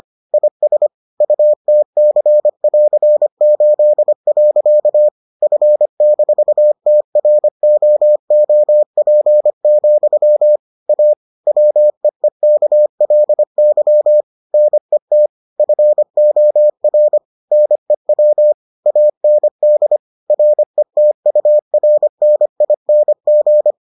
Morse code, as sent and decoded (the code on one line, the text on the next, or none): .. ... / ..- - -.-. .-.-. ---.. .-.-.- / ..-. -....- - .-. --- --- .--. --..-- / .- / .-- . -.- .-.. -.-- / -. - / ..-. --- .-. / -. .-- / .- -. -.. / .-. - ..- .-. -. .. -. --.
IS UTC<AR>8. F-TROOP, A WEEKLY NET FOR NEW AND RETURNING